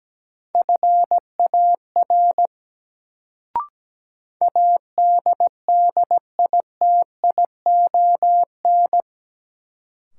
Morse code, heard as a keyed tone, about 17 wpm, 700 Hz. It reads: FAR ADDITION
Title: Sets of 3 Words - New Word Drill - 17wpm